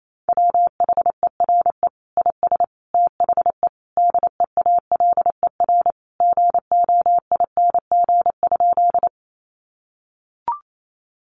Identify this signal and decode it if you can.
Morse code, heard as a keyed tone, about 28 wpm, 700 Hz. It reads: WHERE IS THE DEALER GOING?
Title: Sentences from Top 1100 Words 28wpm